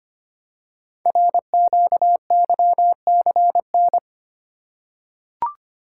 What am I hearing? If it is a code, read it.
Morse code, heard as a keyed tone, about 25 wpm, 700 Hz. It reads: RQYCN
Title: Code Groups - 5 Letters - 25wpm